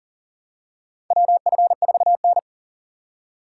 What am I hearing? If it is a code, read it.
Morse code, heard as a keyed tone, about 40 wpm, 700 Hz. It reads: WF4N
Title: USA Call Signs - 6x Character Spacing - 40wpm